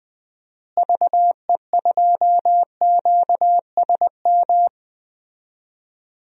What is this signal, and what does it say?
Morse code, heard as a keyed tone, about 20 wpm, 700 Hz. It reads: VE2QSM
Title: DXCC-100 Call Signs - 6 Characters - Repeated 2x - 20wpm